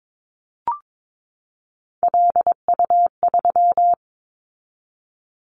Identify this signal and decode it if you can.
Morse code, heard as a keyed tone, about 22 wpm, 700 Hz. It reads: LU3
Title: Callsign Prefixes for Top 50 DXCC - 22wpm